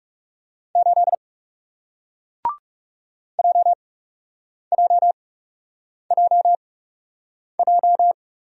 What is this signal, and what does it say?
Morse code, heard as a keyed tone, about 41 wpm, spaced as for 8 wpm, 700 Hz. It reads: 8 JJJJ